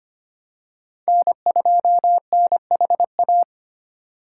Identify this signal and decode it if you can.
Morse code, heard as a keyed tone, about 25 wpm, 700 Hz. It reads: N2NHA
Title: USA Call Signs - 5 Characters - 3x Character Spacing - 25wpm